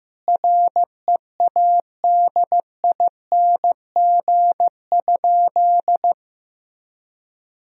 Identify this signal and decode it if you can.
Morse code, heard as a keyed tone, about 15 wpm, 700 Hz. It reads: READING?
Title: Sentences from Top 1000 Words - Review Part 3 - 15wpm